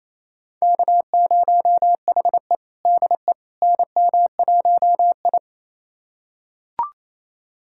Morse code, heard as a keyed tone, about 28 wpm, 700 Hz. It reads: K0HE DE NM1I
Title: USA Callsign DE Callsign - 4 Characters - 28wpm